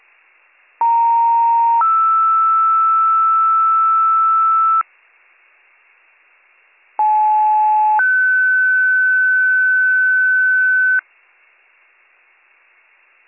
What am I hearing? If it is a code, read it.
Two-tone page, heard as a one-second tone, then a three-second tone, about 930 then 1320 Hz, about 860 then 1500 Hz. It is heard from 0.8 to 11.0 s.